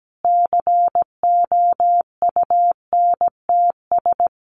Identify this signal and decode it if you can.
Morse code, heard as a keyed tone, about 17 wpm, 700 Hz. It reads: COUNTS
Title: Easy Phrases 17wpm